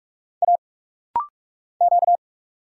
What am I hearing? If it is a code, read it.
Morse code, heard as a keyed tone, about 44 wpm, 700 Hz. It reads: A Q
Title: Single Letter-Number Alt - Rapid-Fire 45wpm